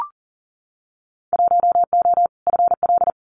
Morse code, heard as a keyed tone, about 40 wpm, 700 Hz. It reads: 1O FL
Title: Winter Field Day Exchanges 40wpm